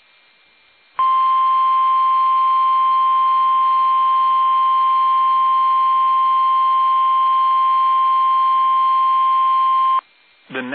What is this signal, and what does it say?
Emergency Alert System (SAME) tones (the 1050 Hz attention tone), with speech. It is heard from 0.9 to 10.0 s.